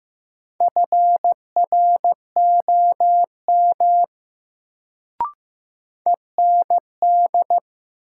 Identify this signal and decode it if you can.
Morse code, heard as a keyed tone, about 15 wpm, 700 Hz. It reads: FROM END
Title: Sets of 2 Words - New Word Drill - 15wpm